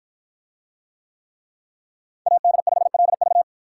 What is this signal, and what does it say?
Morse code, heard as a keyed tone, about 52 wpm, 700 Hz. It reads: AD5LV